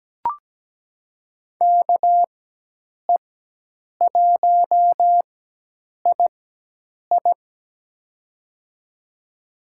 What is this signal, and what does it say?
Morse code, heard as a keyed tone, about 17 wpm, spaced as for 8 wpm, 700 Hz. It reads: KE1II
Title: USA Call Signs - 5 Characters - 4x Character Spacing - 17wpm